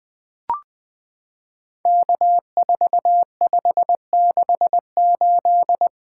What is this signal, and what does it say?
Morse code, heard as a keyed tone, about 20 wpm, 700 Hz. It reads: K4568
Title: POTA QSO Practice - 20wpm